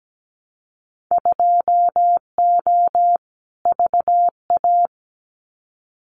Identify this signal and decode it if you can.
Morse code, heard as a keyed tone, about 17 wpm, 700 Hz. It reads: 2O VA
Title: Winter Field Day Exchanges 17wpm